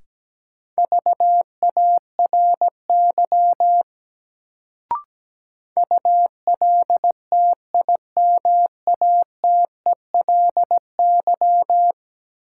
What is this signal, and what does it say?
Morse code, heard as a keyed tone, about 17 wpm, 700 Hz. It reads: VARY ULTIMATELY
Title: Sentences from Top 1200 Words - New Word Drill Alt - 17wpm